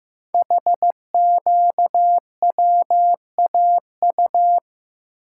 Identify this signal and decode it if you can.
Morse code, heard as a keyed tone, about 15 wpm, 700 Hz. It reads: HQWAU
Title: Code Groups - 5 Letters - 15wpm